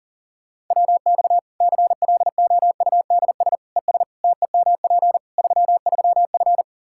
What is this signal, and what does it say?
Morse code, heard as a keyed tone, about 40 wpm, 700 Hz. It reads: WX CLOUDS ES TEMP 33F
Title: K1USN SST QSO Practice 40wpm